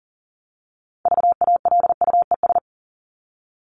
Morse code, heard as a keyed tone, about 40 wpm, 700 Hz. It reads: VALUES
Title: ES:S Suffix - Top 100 Words with Base Words 40wpm